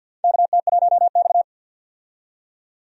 Morse code, heard as a keyed tone, about 46 wpm, 700 Hz. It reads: KT1X
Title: USA Call Signs - 4 Characters - 50wpm